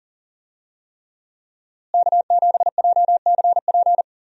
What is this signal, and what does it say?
Morse code, heard as a keyed tone, about 40 wpm, 700 Hz. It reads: K7JCP
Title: USA Call Signs - 5 Characters - Repeated 3x - 40wpm